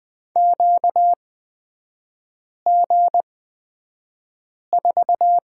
Morse code, heard as a keyed tone, about 20 wpm, spaced as for 5 wpm, 700 Hz. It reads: QG4
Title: 243 Lesson 39e - Letter z - Three Characters 20wpm